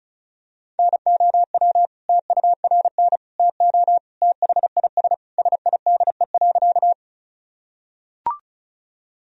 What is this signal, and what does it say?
Morse code, heard as a keyed tone, about 36 wpm, 700 Hz. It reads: NOW TURN TO THIS SIDE.
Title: Sentences from Top 1000 Words - Review Part 1 - 35wpm